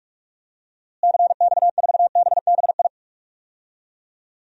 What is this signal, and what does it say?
Morse code, heard as a keyed tone, about 46 wpm, 700 Hz. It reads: CX4BBI
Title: DXCC-100 Call Signs - 6 Characters - Repeated 2x - 45wpm